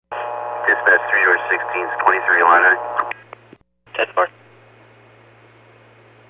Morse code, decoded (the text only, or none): none